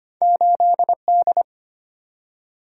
Morse code, heard as a keyed tone, about 25 wpm, 690 Hz. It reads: 8D
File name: Two Letter-Number Combinations - 25wpm